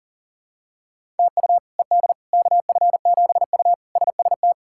Morse code, heard as a keyed tone, about 40 wpm, 700 Hz. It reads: TU ED KF7U SST
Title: K1USN SST QSO Practice - 40wpm